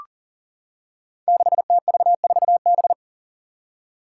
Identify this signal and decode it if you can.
Morse code, heard as a keyed tone, about 40 wpm, 700 Hz. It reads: 6TV4B
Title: Five Challenging Letter-Number Combinations 40wpm